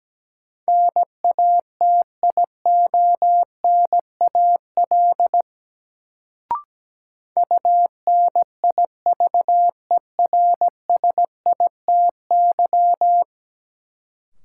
Morse code, heard as a keyed tone, about 17 wpm, 700 Hz. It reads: NATIONAL UNIVERSITY
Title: Sentences from Top 1000 Words - Review Part 4 - New Word Drill - 17wpm